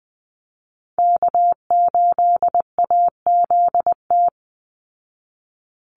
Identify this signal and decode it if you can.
Morse code, heard as a keyed tone, about 20 wpm, 700 Hz. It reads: K8AZT
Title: Call Signs Repeated 2x - 20wpm